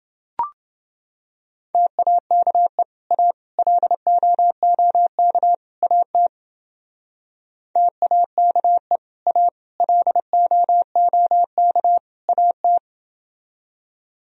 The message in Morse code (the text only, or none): TAKE A LOOK AT TAKE A LOOK AT